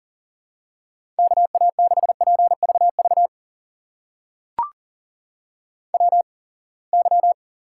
Morse code, heard as a keyed tone, about 40 wpm, spaced as for 28 wpm, 700 Hz. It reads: KA6PVV WY
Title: USA Call Signs - 8x Character Spacing - 40wpm